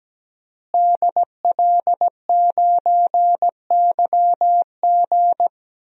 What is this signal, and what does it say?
Morse code, heard as a keyed tone, about 17 wpm, 700 Hz. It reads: DL9YG
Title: DXCC-100 Call Signs - 5 Characters - Repeated 2x - 17wpm